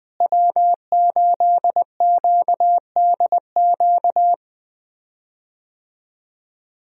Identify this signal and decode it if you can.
Morse code, heard as a keyed tone, about 21 wpm, 700 Hz. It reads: W8QDQ
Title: USA Call Signs - 5 Characters - Repeated 2x - 20wpm